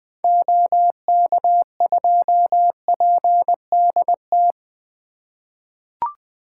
Morse code, heard as a keyed tone, about 20 wpm, 700 Hz. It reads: OK2PDT
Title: 2023 SOTA Activator Call Signs - 20wpm